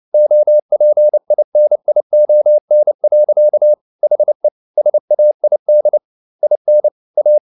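Morse code, heard as a keyed tone, about 29 wpm, 600 Hz. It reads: OPINION. HE SAID IN A
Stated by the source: Morse code at 29 wpm, sending Ken Follett's novel Never in English, from CW med Gnister.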